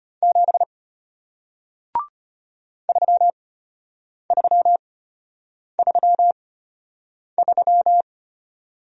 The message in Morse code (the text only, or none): 7 3333